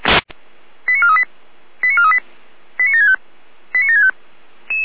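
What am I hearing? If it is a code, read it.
Five-tone selective call (ZVEI), heard as quick runs of five tones, tones of about 80 ms. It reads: 89438 89438 89765 89765
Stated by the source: ZVEI five-tone selective call received via SDR, address 89720.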